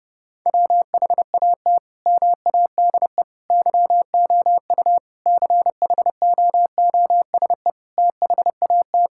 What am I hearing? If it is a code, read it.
Morse code, heard as a keyed tone, about 30 wpm, 700 Hz. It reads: WHAT MADE YOU CHOOSE THAT
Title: Sentences from Top 1000 Words - Review Part 5 - 30wpm